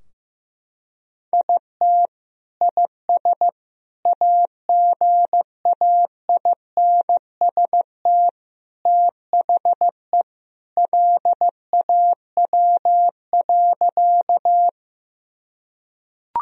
Morse code, heard as a keyed tone, about 15 wpm, 700 Hz. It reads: IT IS AGAINST THE LAW.
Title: Sentences from Top 1000 Words - Review Part 3 - 15wpm